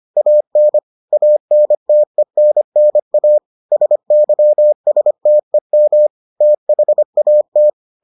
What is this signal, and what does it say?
Morse code, heard as a keyed tone, about 25 wpm, 600 Hz. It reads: AN ANTENNA SYSTEM THAT